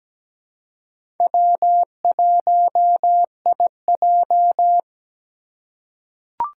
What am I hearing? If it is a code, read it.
Morse code, heard as a keyed tone, about 17 wpm, 700 Hz. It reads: W1IJ
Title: USA Call Signs - 4 Characters - 3x Character Spacing - 17wpm